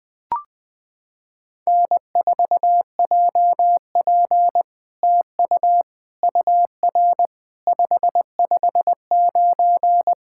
Morse code, heard as a keyed tone, about 20 wpm, 700 Hz. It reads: N4JP TU UR 559
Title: SOTA QSO Practice 20wpm